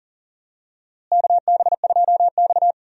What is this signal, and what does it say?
Morse code, heard as a keyed tone, about 40 wpm, 700 Hz. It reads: KB2X
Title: USA Call Signs - 4 Characters - Repeated 3x - 40wpm